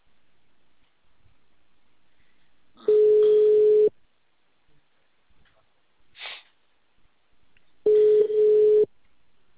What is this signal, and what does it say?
Ringback tone (Europe, 425 Hz).
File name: out- -2004- - - .2250